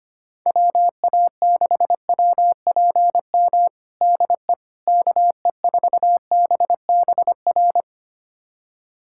Morse code, heard as a keyed tone, about 25 wpm, 700 Hz. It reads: WA6WPM DE KE4BBR
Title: USA Callsign DE Callsign - 6 Characters - 25wpm